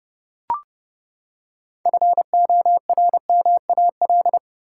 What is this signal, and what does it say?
Morse code, heard as a keyed tone, about 30 wpm, 700 Hz. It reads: FORMAL
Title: Six Letter Words - 30wpm